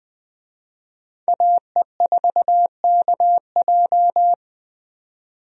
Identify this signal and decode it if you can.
Morse code, heard as a keyed tone, about 21 wpm, 700 Hz. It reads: AE4KJ
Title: USA Call Signs - 5 Characters - 20wpm